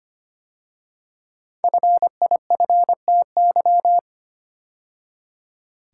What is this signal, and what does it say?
Morse code, heard as a keyed tone, about 25 wpm, 700 Hz. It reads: FIFTY
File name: 5 Letter Words - First 3 Letters Given - 25wpm